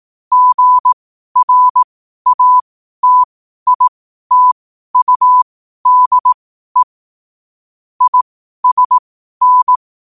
Morse code, heard as a keyed tone, about 18 wpm, spaced as for 12 wpm, 1000 Hz. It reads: GRATITUDE ISN
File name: qotdfiles / 12wpm